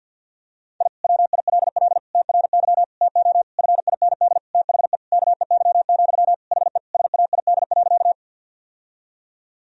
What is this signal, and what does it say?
Morse code, heard as a keyed tone, about 50 wpm, 700 Hz. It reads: I WILL TRY TO FIND THE KEY, HE SAID.
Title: Sentences from Top 1000 Words - Review Part 4 - 50wpm